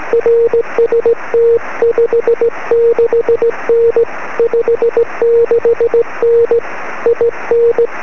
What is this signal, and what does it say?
Morse code, heard as a keyed tone, about 17 wpm, 470 Hz. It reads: RST 56N 56N IN
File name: sm7lcb / 21-pa0ez 23cm